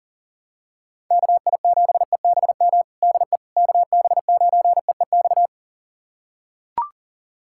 Morse code, heard as a keyed tone, about 40 wpm, 700 Hz. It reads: KI7EBM DE KB9EEX
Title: USA Callsign DE Callsign - 6 Characters - 40wpm